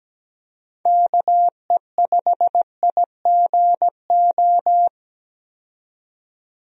Morse code, heard as a keyed tone, about 17 wpm, 700 Hz. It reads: KE5IGO
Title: USA Call Signs - 6 Characters - 17wpm